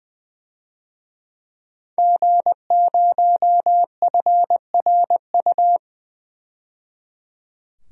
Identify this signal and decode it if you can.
Morse code, heard as a keyed tone, about 21 wpm, 700 Hz. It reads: G0FRU
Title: DXCC-100 Call Signs - 5 Characters - 20wpm